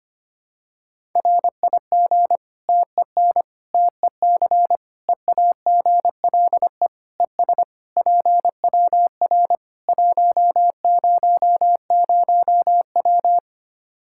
Morse code, heard as a keyed tone, about 25 wpm, 700 Hz. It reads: RIG TEN TEC EAGLE ES PWR 100W
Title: Practice QSO - Exchange 2 - Info - 25wpm